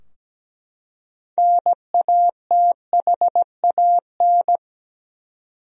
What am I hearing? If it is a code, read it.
Morse code, heard as a keyed tone, about 17 wpm, 700 Hz. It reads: NATHAN
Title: USA Common Names - 17wpm